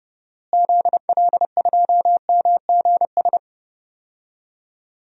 Morse code, heard as a keyed tone, about 31 wpm, 700 Hz. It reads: ZL2MGS
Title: DXCC-100 Call Signs - 6 Characters - Repeated 2x - 30wpm